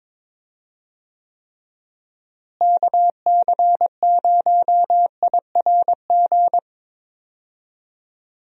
Morse code, heard as a keyed tone, about 22 wpm, 700 Hz. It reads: KC0IRG
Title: USA Call Signs - 6 Characters - Repeated 2x - 22wpm